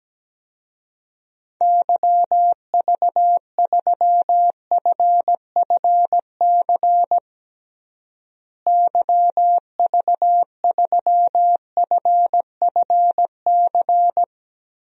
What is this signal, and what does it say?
Morse code, heard as a keyed tone, about 17 wpm, 700 Hz. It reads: YV3FFC YV3FFC
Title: DXCC-100 Call Signs - 6 Characters - Repeated 3x - 17wpm